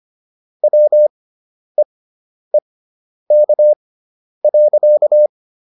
Morse code, heard as a keyed tone, about 25 wpm, spaced as for 10 wpm, 600 Hz. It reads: WEEK.